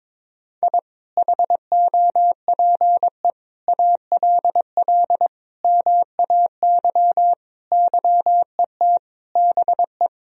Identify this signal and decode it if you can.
Morse code, heard as a keyed tone, about 22 wpm, 700 Hz. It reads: I HOPE ALL MAY YET BE
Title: Sentences from Top 1000 Words - Review Part 1 - 22wpm